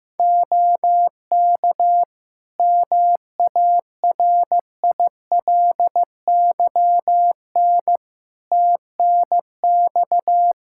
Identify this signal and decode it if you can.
Morse code, heard as a keyed tone, about 15 wpm, 700 Hz. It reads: OK MARILYN TNX
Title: Practice QSO - Exchange 4 - Ending - 15wpm